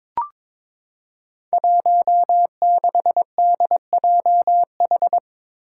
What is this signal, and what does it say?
Morse code, heard as a keyed tone, about 22 wpm, 700 Hz. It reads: 16DJH
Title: Five Challenging Letter-Number Combinations 22wpm